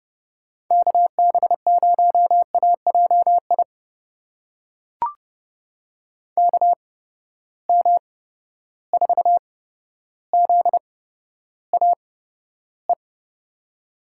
Morse code, heard as a keyed tone, about 30 wpm, spaced as for 13 wpm, 700 Hz. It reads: KB0AJI KM4ZAE
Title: USA Call Signs - 6 Characters - 8x Character Spacing - 30wpm